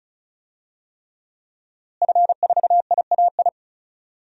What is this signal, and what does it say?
Morse code, heard as a keyed tone, about 36 wpm, 700 Hz. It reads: F4IAI
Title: DXCC-100 Call Signs - 5 Characters - Repeated 3x - 35wpm